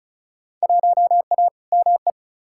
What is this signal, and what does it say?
Morse code, heard as a keyed tone, about 36 wpm, 700 Hz. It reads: 1A ME
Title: Field Day Exchanges Encore - 35wpm